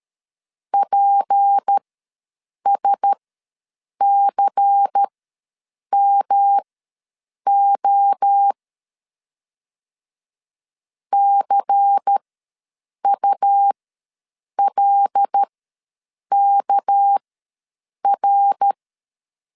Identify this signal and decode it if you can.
Morse code, heard as a keyed tone, about 13 wpm, spaced as for 7 wpm, 800 Hz. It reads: PSCMO CULKR
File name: LEZIONI / 6 PKX0 / 6 PK ws 12wpm 3116